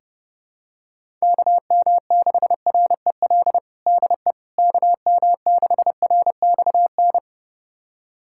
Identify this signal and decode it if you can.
Morse code, heard as a keyed tone, about 30 wpm, 700 Hz. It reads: KM6REL DE KM6RXN
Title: USA Callsign DE Callsign - 6 Characters - 30wpm